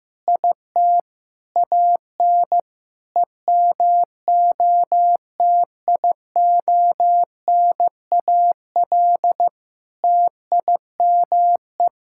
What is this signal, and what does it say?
Morse code, heard as a keyed tone, about 16 wpm, 700 Hz. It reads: IT AN EMOTIONAL TIME
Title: Sentences from Top 1000 Words - Review Part 5 - 15wpm